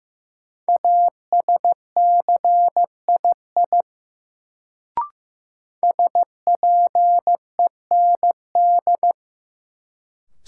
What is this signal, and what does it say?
Morse code, heard as a keyed tone, about 15 wpm, 700 Hz. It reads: ASCII SPEND
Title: Sets of 5 Words - New Word Drill - 15wpm